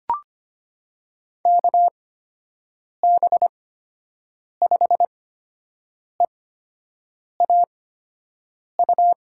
Morse code, heard as a keyed tone, about 25 wpm, spaced as for 7 wpm, 700 Hz. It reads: KB5EAU
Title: USA Call Signs - 6 Characters - 8x Character Spacing - 25wpm